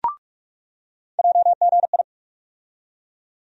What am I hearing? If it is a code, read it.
Morse code, heard as a keyed tone, about 46 wpm, 700 Hz. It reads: JGI
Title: USA Call Sign Postfixes - 3 Letters 45wpm